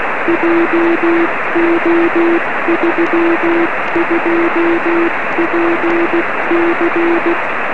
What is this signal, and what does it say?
Morse code, heard as a keyed tone, about 16 wpm, 340 Hz, with noise. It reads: JO32PC
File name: 1908z df1vb-p 13cm